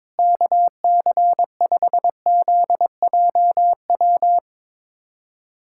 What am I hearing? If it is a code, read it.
Morse code, heard as a keyed tone, about 22 wpm, 700 Hz. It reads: KC5ZJW
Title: USA Callsign DE Callsign 22wpm